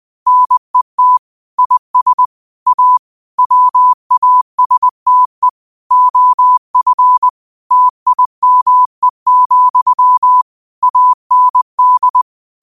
Morse code, heard as a keyed tone, about 20 wpm, 1000 Hz. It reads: NET IS A WASTE OF TIME, AND
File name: qotdfiles / 20wpm